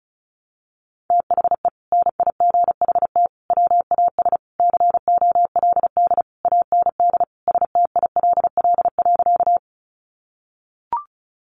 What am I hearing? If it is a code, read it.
Morse code, heard as a keyed tone, about 34 wpm, 700 Hz. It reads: THE NIGHT WAS COLD AND STILL.
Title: Sentences from Top 400 Words 35wpm